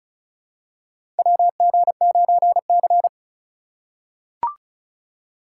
Morse code, heard as a keyed tone, about 36 wpm, 690 Hz. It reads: WG9C
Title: USA Call Signs - 4 Characters - 3x Character Spacing - 35wpm